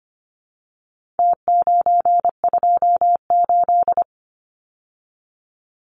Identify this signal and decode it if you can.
Morse code, heard as a keyed tone, about 25 wpm, 700 Hz. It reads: T928
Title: Numbers - Contest Style - 4-Digit 25wpm